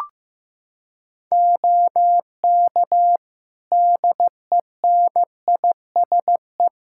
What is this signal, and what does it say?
Morse code, heard as a keyed tone, about 16 wpm, 700 Hz. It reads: OK DENISE
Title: Practice QSO - Exchange 3 - More Info - 15wpm